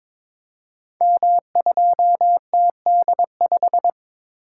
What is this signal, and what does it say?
Morse code, heard as a keyed tone, about 22 wpm, 700 Hz. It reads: M2TD5
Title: Code Groups - 5 Letter:Numbers - 22wpm